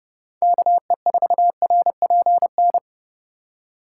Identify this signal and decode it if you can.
Morse code, heard as a keyed tone, about 30 wpm, 700 Hz. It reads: KE4RPN
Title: USA Call Signs - 6 Characters - 2x Character Spacing - 30wpm